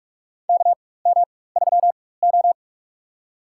Morse code, heard as a keyed tone, about 46 wpm, spaced as for 21 wpm, 700 Hz. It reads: KM3O
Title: USA Call Signs - 4x Character Spacing - 45wpm